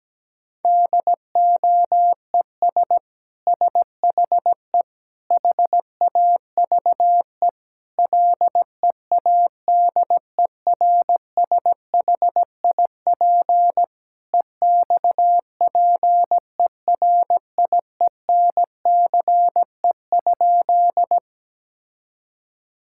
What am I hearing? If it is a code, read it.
Morse code, heard as a keyed tone, about 17 wpm, 700 Hz. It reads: DOES SHE HAVE LEADERSHIP EXPERIENCE?
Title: Sentences from Top 1600 Words 17wpm